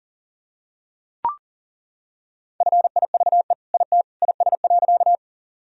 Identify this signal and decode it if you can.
Morse code, heard as a keyed tone, about 40 wpm, 700 Hz. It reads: FIVE IT IS.
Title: Sentences from Top 1000 Words - Review Part 2 - 40wpm